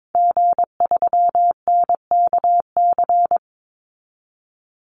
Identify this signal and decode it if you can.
Morse code, heard as a keyed tone, about 22 wpm, 700 Hz. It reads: G3NKC
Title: Call Signs Repeated 2x - 22wpm